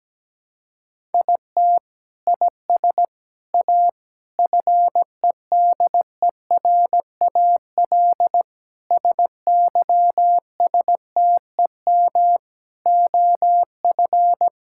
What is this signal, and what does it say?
Morse code, heard as a keyed tone, about 17 wpm, 700 Hz. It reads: IT IS A FEDERAL SYSTEM OF
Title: Sentences from Top 1000 Words - Review Part 5 - 17wpm